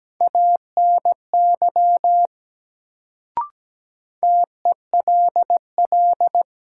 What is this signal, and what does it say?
Morse code, heard as a keyed tone, about 17 wpm, 700 Hz. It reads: ANY TELL
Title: Sentences from Top 200 Words - New Word Drill - 17wpm